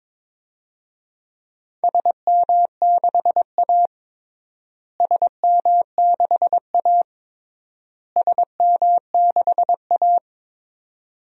Morse code, heard as a keyed tone, about 22 wpm, 700 Hz. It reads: SM6A SM6A SM6A
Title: DXCC-100 Call Signs - 4 Characters - Repeated 3x - 22wpm